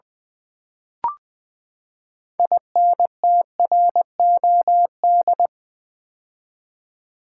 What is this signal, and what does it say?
Morse code, heard as a keyed tone, about 20 wpm, 700 Hz. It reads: INTROD
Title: Build A Word - 6 Letters At A Time - 12-Letter Words - 20wpm